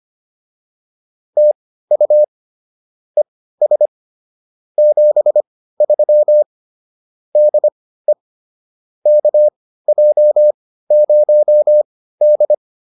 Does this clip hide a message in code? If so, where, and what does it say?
Yes: Morse code from 1.4 to 12.5 s, heard as a keyed tone, about 25 wpm, spaced as for 15 wpm, 600 Hz. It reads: TU ES 73 DE KJ0D